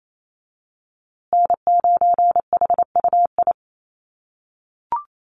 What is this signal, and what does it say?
Morse code, heard as a keyed tone, about 28 wpm, 700 Hz. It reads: N9HUI
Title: USA Call Signs Repeated 3x 28wpm